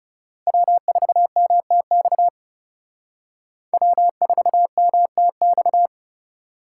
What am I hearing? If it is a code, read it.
Morse code, heard as a keyed tone, about 34 wpm, 700 Hz. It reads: W4MTX W4MTX
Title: Speed Racing - USA 1x3 Call Signs 30wpm